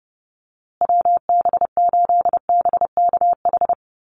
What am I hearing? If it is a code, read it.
Morse code, heard as a keyed tone, about 30 wpm, 700 Hz. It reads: WB8BKH
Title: USA Call Signs Repeated 3x 30wpm